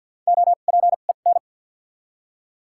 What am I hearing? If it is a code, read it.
Morse code, heard as a keyed tone, about 49 wpm, spaced as for 36 wpm, 700 Hz. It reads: OPEN
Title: 2x Character Spacing - Top 300 Words - 50wpm